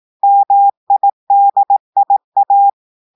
Morse code, heard as a keyed tone, about 18 wpm, 800 Hz. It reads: MIDIA
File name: audios / portugais / portugais texte en clair 1 vitesse 18 mots minutes